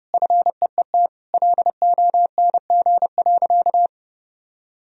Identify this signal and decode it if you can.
Morse code, heard as a keyed tone, about 30 wpm, 700 Hz. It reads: FEET LONG.
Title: Sentences from Top 1000 Words - Review Part 2 - 30wpm